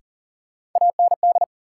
Morse code, heard as a keyed tone, about 40 wpm, 700 Hz. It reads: AND
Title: Sets of 2 Words - New Word Drill - 40wpm